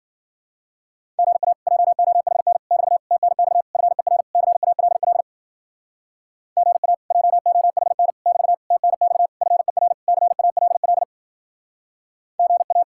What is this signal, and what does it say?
Morse code, heard as a keyed tone, about 57 wpm, 700 Hz. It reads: GA JOHN <BT> TNX FER CALL GA JOHN <BT> TNX FER CALL GA